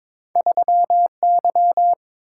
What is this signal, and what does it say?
Morse code, heard as a keyed tone, about 22 wpm, 700 Hz. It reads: 3Y
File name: practice / 22wpm / Two Letter-Number Combinations - 22wpm